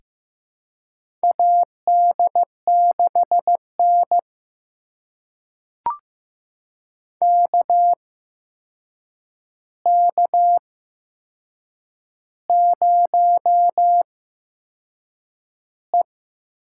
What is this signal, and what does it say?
Morse code, heard as a keyed tone, about 15 wpm, spaced as for 6 wpm, 700 Hz. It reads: AD6N KK0E
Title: USA Call Signs - 4 Characters - 8x Character Spacing - 15wpm